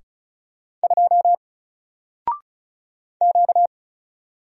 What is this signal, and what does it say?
Morse code, heard as a keyed tone, about 36 wpm, 700 Hz. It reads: 2 Q
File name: Single-Letter-Number-35wpm